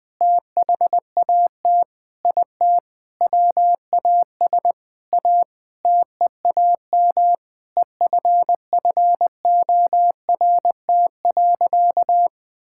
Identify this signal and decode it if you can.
Morse code, heard as a keyed tone, about 21 wpm, 700 Hz. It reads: THAT IT WAS A TEAM EFFORT.
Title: Sentences with 8-Word Phrases - 20wpm